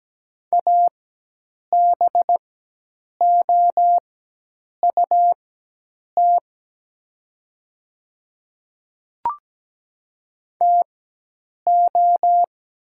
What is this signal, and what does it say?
Morse code, heard as a keyed tone, about 17 wpm, spaced as for 8 wpm, 700 Hz. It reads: ABOUT TO
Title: 4x Character Spacing - Top 100 Words - 17wpm